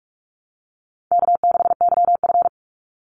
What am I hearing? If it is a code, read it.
Morse code, heard as a keyed tone, about 44 wpm, 700 Hz. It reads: K6YF
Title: Easy Call Signs Repeated 3x - 45wpm